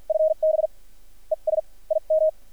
Morse code, heard as a keyed tone, about 50 wpm, spaced as for 40 wpm, 630 Hz. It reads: FB ES IM
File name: archived / CRSnet